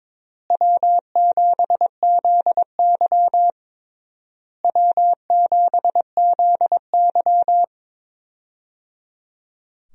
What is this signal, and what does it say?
Morse code, heard as a keyed tone, about 22 wpm, 700 Hz. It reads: W7ZY W7ZY
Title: USA Call Signs - 4 Characters - Repeated 3x - 22wpm